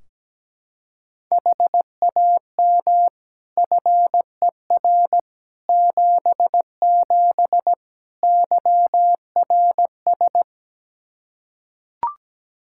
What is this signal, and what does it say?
Morse code, heard as a keyed tone, about 17 wpm, 700 Hz. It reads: HAM FER 77 YRS
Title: K1USN SST QSO Practice 17wpm